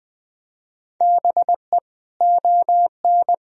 Morse code, heard as a keyed tone, about 20 wpm, 700 Hz. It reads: BE ON
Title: Sentences from Top 1000 Words - Review Part 1 - 20wpm